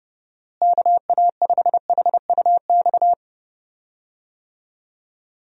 Morse code, heard as a keyed tone, about 30 wpm, 700 Hz. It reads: KA5HUX
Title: USA Call Signs - 6 Characters - Repeated 2x - 30wpm